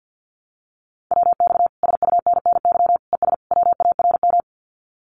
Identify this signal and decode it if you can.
Morse code, heard as a keyed tone, about 50 wpm, 700 Hz. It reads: WX SUNNY ES WARM
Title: Common QSO Phrases Encore 50wpm